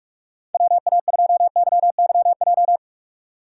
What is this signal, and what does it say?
Morse code, heard as a keyed tone, about 46 wpm, 700 Hz. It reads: WA2YYJ
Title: USA Call Signs - 6 Characters - 2x Character Spacing - 45wpm